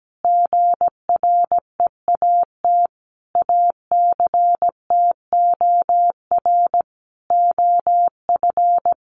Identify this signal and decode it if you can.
Morse code, heard as a keyed tone, about 17 wpm, 700 Hz. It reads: GREAT ACTOR OF
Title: Sentences from Top 1500 Words 17wpm